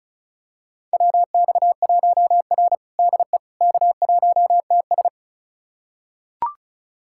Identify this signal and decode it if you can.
Morse code, heard as a keyed tone, about 34 wpm, 700 Hz. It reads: WX1R DE K1TS
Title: USA Callsign DE Callsign - 4 Characters - 35wpm